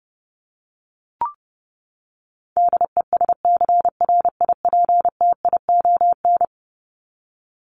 Morse code, heard as a keyed tone, about 30 wpm, 700 Hz. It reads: DESCRIPTION
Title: Top 1001-1250 Words 30wpm